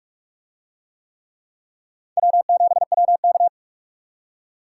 Morse code, heard as a keyed tone, about 44 wpm, 700 Hz. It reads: W7WK